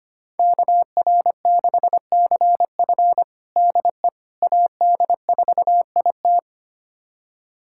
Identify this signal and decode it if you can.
Morse code, heard as a keyed tone, about 25 wpm, 700 Hz. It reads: KR6CF DE AD4IT
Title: USA Callsign DE Callsign - 5 Characters - 25wpm